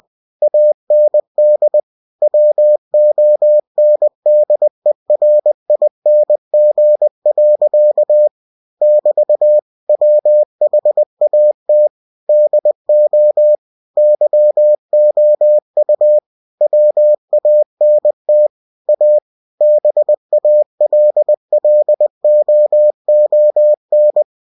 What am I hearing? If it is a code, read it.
Morse code, heard as a keyed tone, about 20 wpm, 600 Hz. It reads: AND WONDERING. <BT> WHAT DO YOU WANT A BALLOON
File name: cw-books / winnie-the-pooh / winnie-20wpm-page-0005